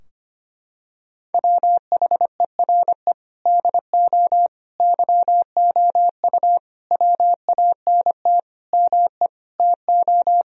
Morse code, heard as a keyed tone, about 25 wpm, 700 Hz. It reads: WHERE DO YOU WANT ME TO
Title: Sentences from Top 1000 Words - Review Part 2 - 25wpm